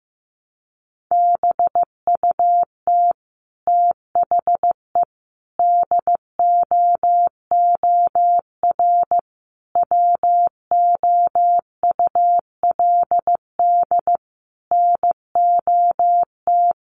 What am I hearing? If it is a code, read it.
Morse code, heard as a keyed tone, about 15 wpm, 700 Hz. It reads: BUT THE DOOR WOULD NOT
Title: Sentences from Top 600 Words 15wpm